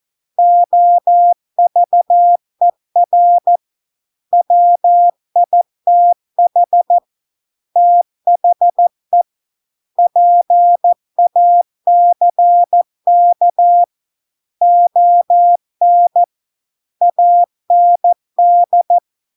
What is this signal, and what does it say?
Morse code, heard as a keyed tone, about 14 wpm, 700 Hz. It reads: OVER WITH THE PACK ON AND